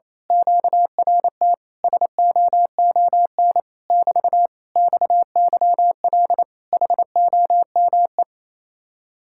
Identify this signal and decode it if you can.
Morse code, heard as a keyed tone, about 27 wpm, 700 Hz. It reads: QRT SOON <BT> XYL HOME
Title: Speed-Racing Common QSO Phrases Encore - 28wpm